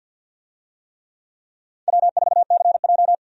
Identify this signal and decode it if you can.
Morse code, heard as a keyed tone, about 52 wpm, 700 Hz. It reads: W4CJ
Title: USA Call Signs - 4 Characters - 50wpm